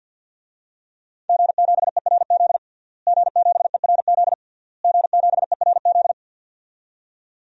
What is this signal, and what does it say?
Morse code, heard as a keyed tone, about 47 wpm, 700 Hz. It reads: G7ERZ G7ERZ G7ERZ
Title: DXCC-100 Call Signs - 5 Characters - Repeated 3x - 50wpm